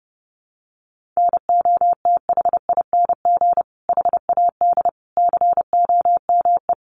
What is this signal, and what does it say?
Morse code, heard as a keyed tone, about 30 wpm, 700 Hz. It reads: NOTHING HAD COME
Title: Sentences from Top 500 Words 30wpm